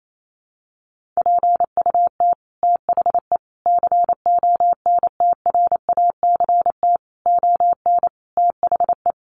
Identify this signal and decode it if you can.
Morse code, heard as a keyed tone, about 28 wpm, 700 Hz. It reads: PUT THE CONTRACT ON THE
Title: Sentences from Top 1400 Words 28wpm